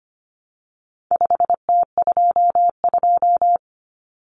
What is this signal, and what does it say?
Morse code, heard as a keyed tone, about 25 wpm, 700 Hz. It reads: 5T22
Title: Common Cut Numbers - 4-Digit 25wpm